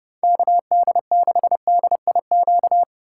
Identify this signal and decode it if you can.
Morse code, heard as a keyed tone, about 34 wpm, 700 Hz. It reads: KD6DIQ